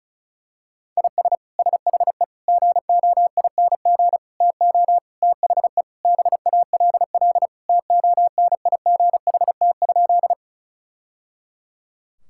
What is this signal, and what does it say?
Morse code, heard as a keyed tone, about 36 wpm, 700 Hz. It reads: IS SHE GOING TO THE BALL TONIGHT?
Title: Sentences from Top 1000 Words - Review Part 6 - 35wpm